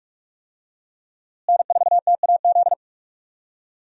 Morse code, heard as a keyed tone, about 46 wpm, 700 Hz. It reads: N4TAZ
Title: USA Call Signs - 5 Characters - Repeated 2x - 45wpm